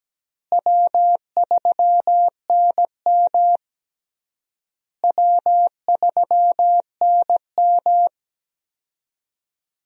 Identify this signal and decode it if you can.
Morse code, heard as a keyed tone, about 17 wpm, 700 Hz. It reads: W3NM W3NM
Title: USA Call Signs - 4 Characters - Repeated 3x - 17wpm